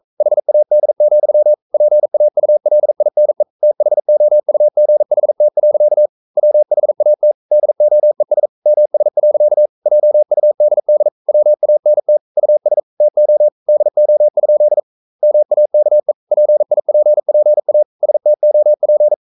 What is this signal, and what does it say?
Morse code, heard as a keyed tone, about 42 wpm, 600 Hz. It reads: HAD, PAULINE THOUGHT. WHAT DOES MS. JUDD WANT US TO DO? MAKE PIPPA STOP
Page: Romanen Never af Ken Follet på engelsk.